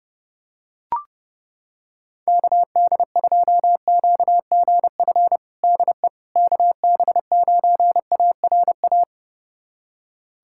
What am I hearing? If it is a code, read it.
Morse code, heard as a keyed tone, about 30 wpm, 700 Hz. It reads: KD2QGF DE KB9ARA
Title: USA Callsign DE Callsign - 6 Characters - 30wpm